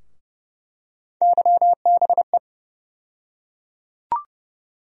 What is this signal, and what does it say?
Morse code, heard as a keyed tone, about 30 wpm, 700 Hz. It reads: YBE
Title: USA Call Sign Postfixes - 3 Letters 30wpm